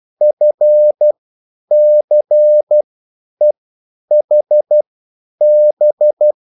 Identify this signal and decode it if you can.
Morse code, heard as a keyed tone, about 12 wpm, spaced as for 9 wpm, 600 Hz. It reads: FCEHB
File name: QSL / CW lesson